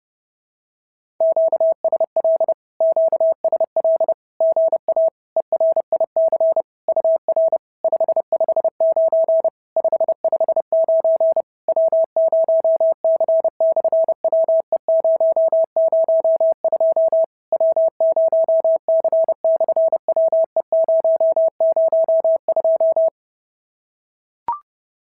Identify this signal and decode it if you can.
Morse code, heard as a keyed tone, about 30 wpm, 650 Hz. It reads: QSL QSL GA ERIC UR 559 559 W0C/WE002 W0C/WE002
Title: SOTA QSO Practice 30wpm